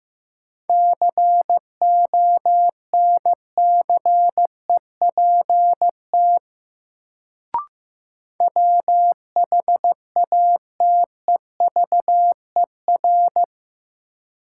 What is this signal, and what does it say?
Morse code, heard as a keyed tone, about 15 wpm, 700 Hz. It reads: CONCEPT WHATEVER
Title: Sentences from Top 1100 Words - New Word Drill - 15wpm